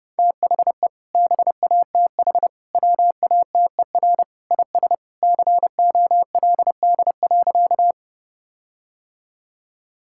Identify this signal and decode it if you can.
Morse code, heard as a keyed tone, about 30 wpm, 700 Hz. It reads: THE BATH WATER IS COLD.
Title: Sentences from Top 1600 Words 30wpm